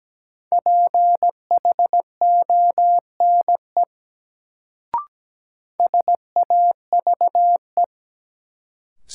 Morse code, heard as a keyed tone, about 17 wpm, 700 Hz. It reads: PHONE SAVE
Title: Sentences from Top 1100 Words - New Word Drill - 17wpm